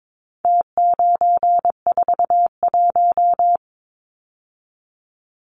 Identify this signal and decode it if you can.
Morse code, heard as a keyed tone, about 22 wpm, 700 Hz. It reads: T941
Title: Numbers - Contest Style - 4-Digit 22wpm